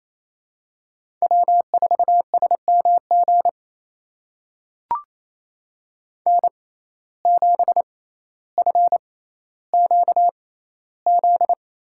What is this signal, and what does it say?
Morse code, heard as a keyed tone, about 28 wpm, spaced as for 14 wpm, 700 Hz. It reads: W4SMG N7FQZ
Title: USA Call Signs - 5 Characters - 6x Character Spacing - 28wpm